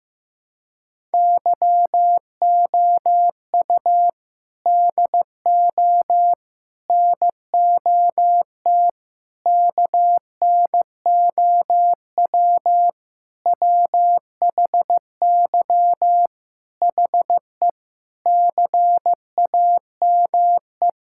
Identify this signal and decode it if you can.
Morse code, heard as a keyed tone, about 15 wpm, 700 Hz. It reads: YOU DO NOT KNOW WHY HE CAME
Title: Sentences from Top 1000 Words - Review Part 6 - 15wpm